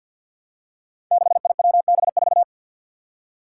Morse code, heard as a keyed tone, about 48 wpm, 700 Hz. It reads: 6IWB4
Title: Five Challenging Letter-Number Combinations 50wpm